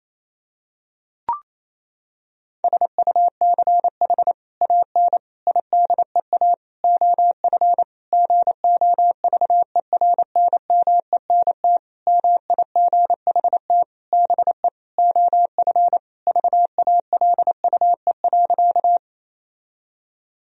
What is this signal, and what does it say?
Morse code, heard as a keyed tone, about 27 wpm, 700 Hz. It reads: SUCH AN IDEA OF GOVERNMENT MIGHT BE OF VALUE.
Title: Sentences from Top 1000 Words - Review Part 3 - 28wpm